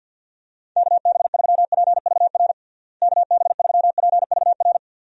Morse code, heard as a keyed tone, about 48 wpm, 700 Hz. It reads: KB3PVR KB3PVR
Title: USA Call Signs - 6 Characters - Repeated 2x - 50wpm